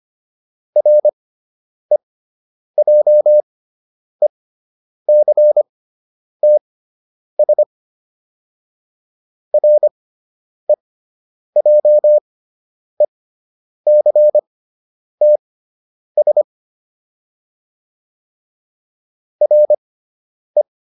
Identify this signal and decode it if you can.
Morse code, heard as a keyed tone, about 25 wpm, spaced as for 9 wpm, 600 Hz. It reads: REJECTS REJECTS RE